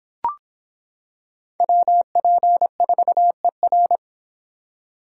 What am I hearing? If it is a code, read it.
Morse code, heard as a keyed tone, about 26 wpm, 700 Hz. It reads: WP4ER